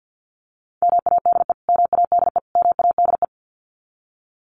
Morse code, heard as a keyed tone, about 50 wpm, 700 Hz. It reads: MADE MADE MADE
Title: Top 300 Words Repeated 50wpm